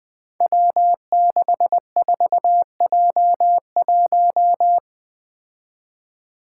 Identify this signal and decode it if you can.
Morse code, heard as a keyed tone, about 20 wpm, 700 Hz. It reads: W64J1
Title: Five Challenging Letter-Number Combinations 20wpm